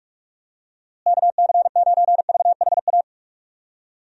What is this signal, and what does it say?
Morse code, heard as a keyed tone, about 46 wpm, 700 Hz. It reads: KC9VHA
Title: USA Call Signs - 6 Characters - Repeated 3x - 45wpm